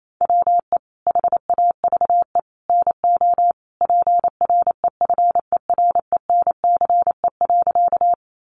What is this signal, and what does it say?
Morse code, heard as a keyed tone, about 28 wpm, 700 Hz. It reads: WE HAVE NO PREFERENCE.
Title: Sentences from Top 1500 Words 28wpm